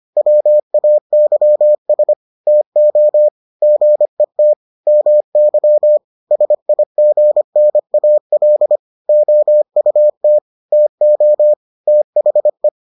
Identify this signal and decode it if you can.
Morse code, heard as a keyed tone, about 25 wpm, 600 Hz. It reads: WAYS TO GET MY SIGNAL OUT TO THE